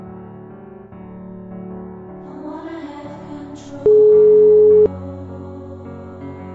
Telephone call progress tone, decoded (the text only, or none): ringback tone